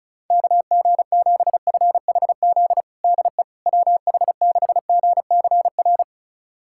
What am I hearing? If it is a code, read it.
Morse code, heard as a keyed tone, about 36 wpm, 700 Hz. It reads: KG7FHZ DE WH6GCR
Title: USA Callsign DE Callsign - 6 Characters - 35wpm